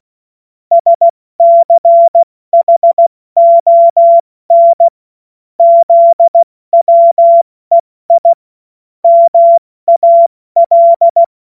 Morse code, heard as a keyed tone, about 16 wpm, spaced as for 14 wpm, 680 Hz. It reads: SCHON ZWEI MAL
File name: Morsekurs / Witze / Witz 25. Januar 2022, 70 F80